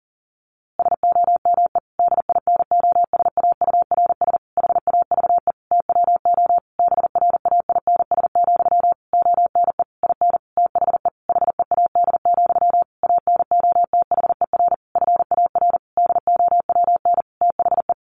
Morse code, heard as a keyed tone, about 40 wpm, 700 Hz. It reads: SOME DINOSAURS HAVE TWO BRAINS, ONE IN THE HEAD, ANOTHER FAR DOWN THE
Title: A Sound Of Thunder 40wpm